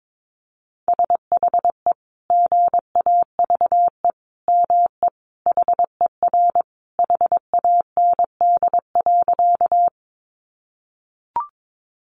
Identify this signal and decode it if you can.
Morse code, heard as a keyed tone, about 22 wpm, 700 Hz. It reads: SHE GAVE ME HER HAND.
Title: Sentences from Top 500 Words 22wpm